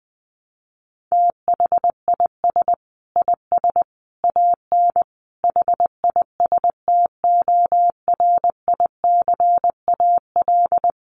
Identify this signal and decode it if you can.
Morse code, heard as a keyed tone, about 20 wpm, 700 Hz. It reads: THIS IS AN HISTORICAL
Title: Sentences from Top 1300 Words 20wpm